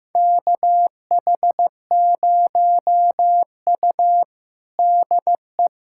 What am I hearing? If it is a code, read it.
Morse code, heard as a keyed tone, about 15 wpm, 700 Hz. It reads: KH0U DE
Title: USA Callsign DE Callsign - 4 Characters - 15wpm